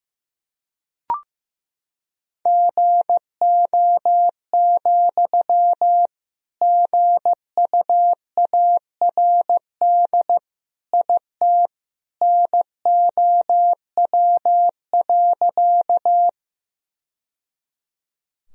Morse code, heard as a keyed tone, about 15 wpm, 700 Hz. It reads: GO, GUARD IT NOW.
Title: Sentences from Top 1000 Words - Review Part 4 - 15wpm